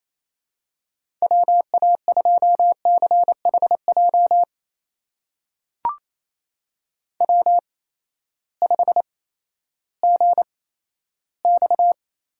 Morse code, heard as a keyed tone, about 28 wpm, spaced as for 14 wpm, 700 Hz. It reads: WA2CHJ W5GX
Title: USA Call Signs - 8x Character Spacing - 28wpm